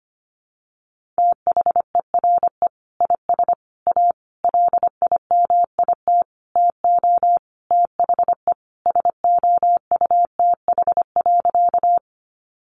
Morse code, heard as a keyed tone, about 25 wpm, 700 Hz. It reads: THERE IS A LIMIT TO THE SOUTH.
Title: Sentences from Top 1500 Words 25wpm